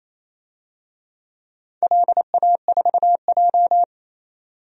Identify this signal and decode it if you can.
Morse code, heard as a keyed tone, about 27 wpm, 700 Hz. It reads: LA4J
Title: DXCC-100 Call Signs - 4 Characters - Repeated 2x - 28wpm